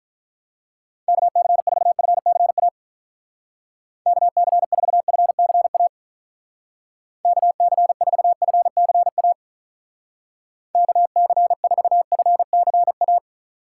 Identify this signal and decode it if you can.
Morse code, heard as a keyed tone, about 46 wpm, 700 Hz. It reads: KC4FCA KC4FCA KC4FCA KC4FCA